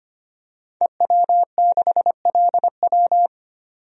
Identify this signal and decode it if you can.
Morse code, heard as a keyed tone, about 25 wpm, 700 Hz. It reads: EW6LW
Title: DXCC-100 Call Signs - 5 Characters - 25wpm